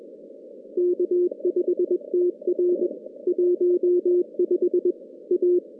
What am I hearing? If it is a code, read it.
Morse code, heard as a keyed tone, about 21 wpm, 360 Hz. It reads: K5TR 15 A
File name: contests / CW-SS-k5tr- -2100